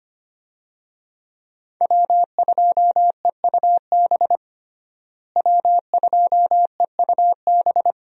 Morse code, heard as a keyed tone, about 25 wpm, 690 Hz. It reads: W2EUB W2EUB
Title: USA Call Signs - 5 Characters - Repeated 2x - 25wpm